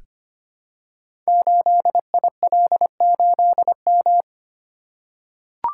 Morse code, heard as a keyed tone, about 26 wpm, 700 Hz. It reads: 8IL8M
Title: Code Groups - 5 Letter:Numbers - 25wpm